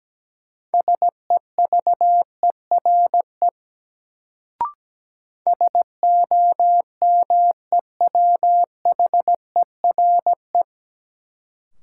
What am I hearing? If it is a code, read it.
Morse code, heard as a keyed tone, about 17 wpm, 700 Hz. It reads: SEVERE SOMEWHERE
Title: Sentences from Top 1400 Words - New Word Drill Alt - 17wpm